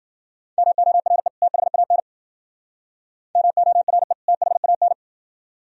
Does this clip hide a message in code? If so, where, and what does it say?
Yes: Morse code from 0.6 to 4.9 s, heard as a keyed tone, about 50 wpm, 700 Hz. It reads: MORE THAN MORE THAN